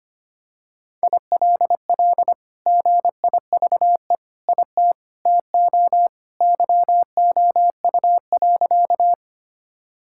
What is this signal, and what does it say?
Morse code, heard as a keyed tone, about 25 wpm, 700 Hz. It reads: ILL GIVE IT TO YOU.
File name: practice / 25wpm / Sentences with 2-Word Phrases - 25wpm